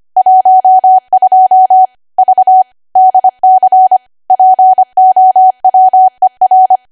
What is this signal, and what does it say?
Morse code, heard as a keyed tone, about 25 wpm, 750 Hz. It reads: 12 V DC POWER
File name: cwa / W1AW-Level3 / 25WPM